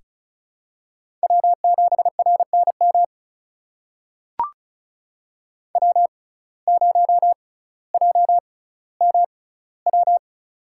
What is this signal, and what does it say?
Morse code, heard as a keyed tone, about 34 wpm, spaced as for 18 wpm, 700 Hz. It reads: W7RNM W0JMW
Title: USA Call Signs - 5 Characters - 6x Character Spacing - 35wpm